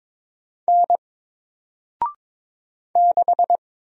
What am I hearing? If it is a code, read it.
Morse code, heard as a keyed tone, about 22 wpm, 700 Hz. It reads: N 6
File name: Single-Letter-Number-22wpm